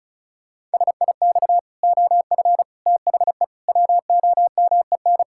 Morse code, heard as a keyed tone, about 36 wpm, 700 Hz. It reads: SIX OF THE WOMEN
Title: Pangram Challenge v2 - 35wpm